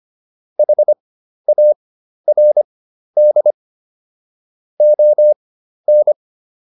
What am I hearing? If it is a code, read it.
Morse code, heard as a keyed tone, about 25 wpm, spaced as for 12 wpm, 600 Hz. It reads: HARD ON